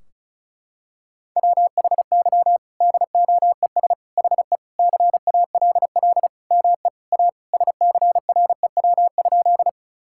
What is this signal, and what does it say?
Morse code, heard as a keyed tone, about 36 wpm, 700 Hz. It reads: WHY DOES HE CALL ME A SCREW?
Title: Sentences from Top 1600 Words 35wpm